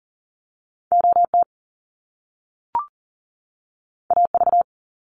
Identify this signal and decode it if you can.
Morse code, heard as a keyed tone, about 40 wpm, 700 Hz. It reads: OT AV
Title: Top Two Letter Combinations in Words 40wpm